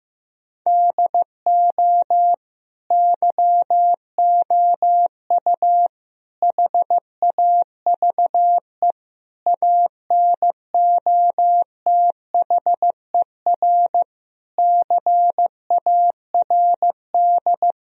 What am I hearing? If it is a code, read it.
Morse code, heard as a keyed tone, about 16 wpm, 700 Hz. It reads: DO YOU HAVE ANOTHER CARD
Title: Sentences from Top 1000 Words - Review Part 4 - 15wpm